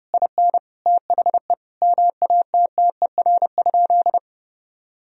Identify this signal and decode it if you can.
Morse code, heard as a keyed tone, about 30 wpm, 700 Hz. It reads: IN THE MATTER?
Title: Sentences from Top 1000 Words - Review Part 1 - 30wpm